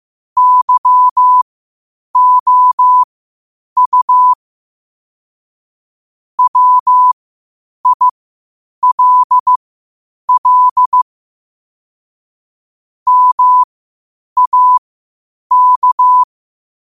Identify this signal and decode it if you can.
Morse code, heard as a keyed tone, about 15 wpm, spaced as for 8 wpm, 1000 Hz. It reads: YOU WILL MAK
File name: qotdfiles / 8wpm